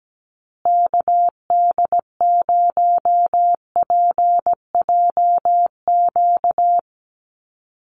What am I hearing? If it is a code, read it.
Morse code, heard as a keyed tone, about 17 wpm, 700 Hz. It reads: KD0PJQ
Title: USA Call Signs Repeated 2x 17wpm